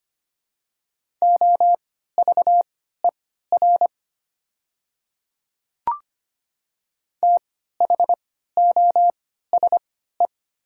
Morse code, heard as a keyed tone, about 25 wpm, spaced as for 14 wpm, 700 Hz. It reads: OVER THOSE
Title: 3x Character Spacing - Top 300 Words - 25wpm